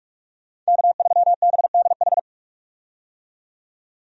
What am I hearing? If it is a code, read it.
Morse code, heard as a keyed tone, about 46 wpm, 700 Hz. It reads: K3BDH
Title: USA Call Signs - 5 Characters - 45wpm